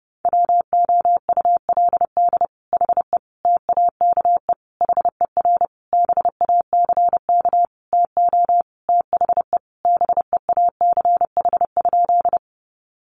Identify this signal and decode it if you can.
Morse code, heard as a keyed tone, about 30 wpm, 700 Hz. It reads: WOULD HE TAKE HER BACK TO THE BEACH?
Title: Sentences from Top 1500 Words 30wpm